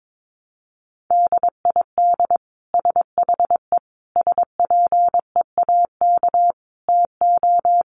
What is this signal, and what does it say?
Morse code, heard as a keyed tone, about 22 wpm, 700 Hz. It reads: DID SHE SPEAK TO
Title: Sentences from Top 1200 Words 22wpm